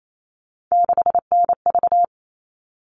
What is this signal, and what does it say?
Morse code, heard as a keyed tone, about 28 wpm, 700 Hz. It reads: 6NV
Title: All Cut Numbers - 3-Digit 28wpm